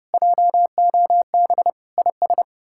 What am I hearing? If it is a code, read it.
Morse code, heard as a keyed tone, about 37 wpm, 700 Hz. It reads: JOB IS
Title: Speed-Racing Common QSO Phrases - 30wpm